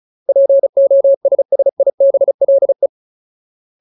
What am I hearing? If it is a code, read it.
Morse code, heard as a keyed tone, about 34 wpm, 550 Hz. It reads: POSSIBLE
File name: Ultimate Word List / 35WPM Ultimate